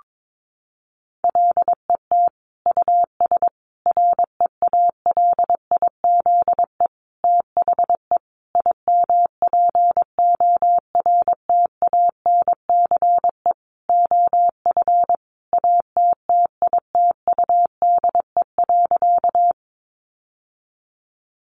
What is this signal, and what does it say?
Morse code, heard as a keyed tone, about 22 wpm, 700 Hz. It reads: LET US REALIZE THE IMPORTANCE OF ATTITUDE.
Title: Sentences from Top 1300 Words 22wpm